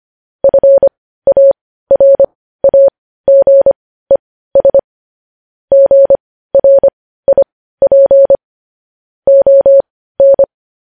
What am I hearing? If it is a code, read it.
Morse code, heard as a keyed tone, about 25 wpm, spaced as for 15 wpm, 550 Hz. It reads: FARAGES GRIP ON
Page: News Headlines in Morse Code at 15 WPM